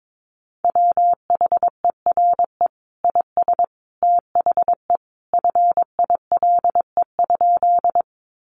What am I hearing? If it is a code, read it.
Morse code, heard as a keyed tone, about 22 wpm, 700 Hz. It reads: WHERE IS THE FILE?
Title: Sentences from Top 1200 Words 22wpm